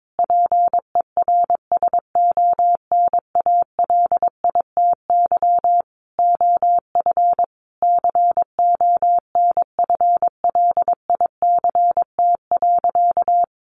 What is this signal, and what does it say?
Morse code, heard as a keyed tone, about 22 wpm, 700 Hz. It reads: PERSONALITY OF CONFLICT.
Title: Sentences from Top 1400 Words 22wpm